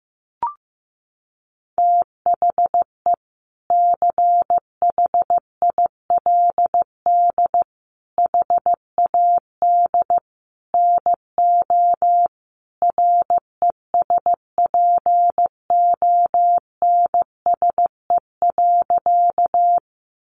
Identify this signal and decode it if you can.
Morse code, heard as a keyed tone, about 15 wpm, 700 Hz. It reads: THE CHILD HAD NO RESPONSE.
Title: Sentences from Top 1400 Words 15wpm